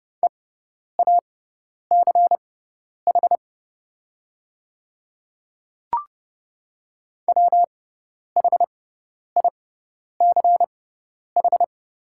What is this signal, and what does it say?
Morse code, heard as a keyed tone, about 30 wpm, spaced as for 10 wpm, 700 Hz. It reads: EACH WHICH
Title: 6x Character Spacing - Top 100 Words - 30wpm